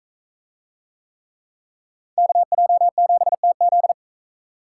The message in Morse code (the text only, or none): KJ7TZ